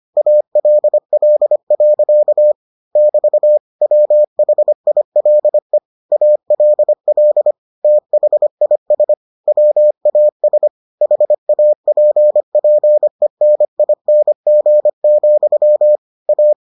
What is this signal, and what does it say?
Morse code, heard as a keyed tone, about 25 wpm, 600 Hz. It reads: ALL. <BT> WHILE ALL THIS WAS HAPPENING, A